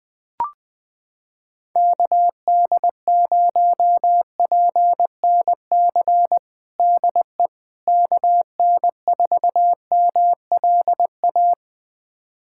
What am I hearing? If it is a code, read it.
Morse code, heard as a keyed tone, about 20 wpm, 700 Hz. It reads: KD0PNC DE KN4MLA
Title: USA Callsign DE Callsign - 6 Characters - 20wpm